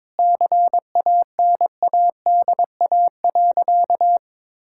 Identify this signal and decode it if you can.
Morse code, heard as a keyed tone, about 21 wpm, 700 Hz. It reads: CANADA.
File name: practice / 22wpm / Sentences with 4-Word Phrases - 22wpm